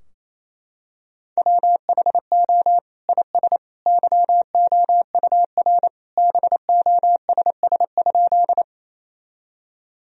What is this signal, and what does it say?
Morse code, heard as a keyed tone, about 28 wpm, 700 Hz. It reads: WHO IS YOUR BOSS?
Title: Sentences from Top 1000 Words - Review Part 4 - 28wpm